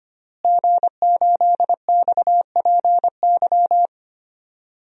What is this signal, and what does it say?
Morse code, heard as a keyed tone, about 25 wpm, 700 Hz. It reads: G8XPY
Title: DXCC-100 Call Signs - 5 Characters - Repeated 3x - 25wpm